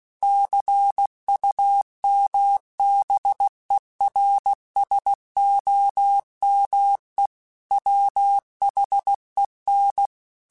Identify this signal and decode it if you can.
Morse code, heard as a keyed tone, about 16 wpm, 800 Hz. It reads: CUMBERSOME WHEN
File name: Technical Subjects / Technical Subjects-mp3s / AND Gate @15